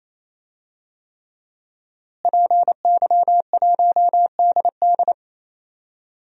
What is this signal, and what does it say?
Morse code, heard as a keyed tone, about 28 wpm, 690 Hz. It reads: PY1DD
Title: DXCC-100 Call Signs - 5 Characters - Repeated 3x - 28wpm